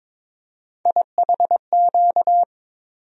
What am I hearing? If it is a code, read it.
Morse code, heard as a keyed tone, about 22 wpm, 700 Hz. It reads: IHQ
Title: USA Call Sign Postfixes - 3 Letters 22wpm